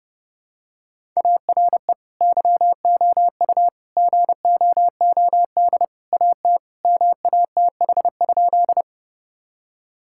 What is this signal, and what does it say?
Morse code, heard as a keyed tone, about 32 wpm, 700 Hz. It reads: ARE YOU GOOD AT MATH?
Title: Sentences from Top 1000 Words - Review Part 6 - 30wpm